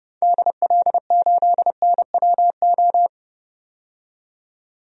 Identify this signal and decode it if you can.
Morse code, heard as a keyed tone, about 29 wpm, 700 Hz. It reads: DL8NWO
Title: DXCC-100 Call Signs - 6 Characters - 30wpm